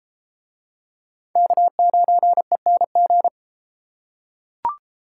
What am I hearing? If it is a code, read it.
Morse code, heard as a keyed tone, about 28 wpm, 700 Hz. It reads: K9ENG